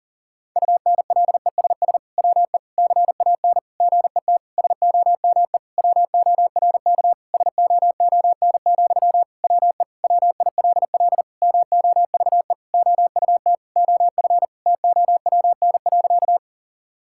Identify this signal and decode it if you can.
Morse code, heard as a keyed tone, about 40 wpm, 700 Hz. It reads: UNLESS WE CAN GET SOME WORK SOON, WE WILL MOVE OUT OF TOWN.
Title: Sentences from Top 1000 Words - Review Part 6 - 40wpm